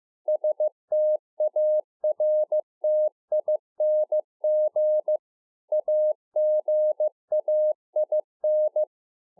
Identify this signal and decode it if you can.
Morse code, heard as a keyed tone, about 15 wpm, 610 Hz. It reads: STARTING AGAIN